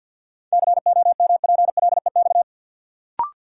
Morse code, heard as a keyed tone, about 52 wpm, 700 Hz. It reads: COMPLEX
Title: Sentences from Top 900 Words - New Word Drill - 50wpm